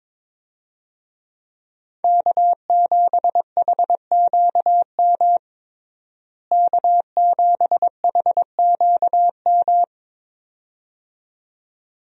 Morse code, heard as a keyed tone, about 21 wpm, 700 Hz. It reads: K7HQM K7HQM
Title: USA Call Signs - 5 Characters - Repeated 2x - 22wpm